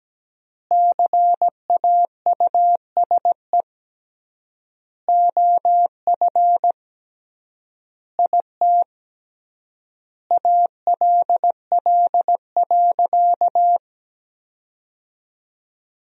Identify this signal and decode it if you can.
Morse code, heard as a keyed tone, about 17 wpm, 700 Hz. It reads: CAUSE OF IT ALL.
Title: 3x Word-Spacing - Sentences from Top 200 Words - 17wpm